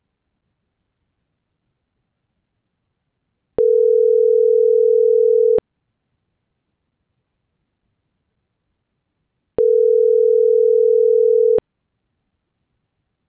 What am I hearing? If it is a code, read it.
Ringback tone (North America), 440 + 480 Hz.